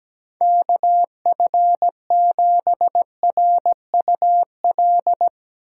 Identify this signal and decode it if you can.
Morse code, heard as a keyed tone, about 17 wpm, 700 Hz. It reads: KF7RUL
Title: USA Call Signs - 6 Characters - 8x Character Spacing - 17wpm